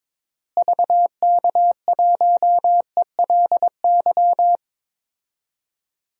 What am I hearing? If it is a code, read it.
Morse code, heard as a keyed tone, about 22 wpm, 700 Hz. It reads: VK1ELY
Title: DXCC-100 Call Signs - 6 Characters - Repeated 3x - 22wpm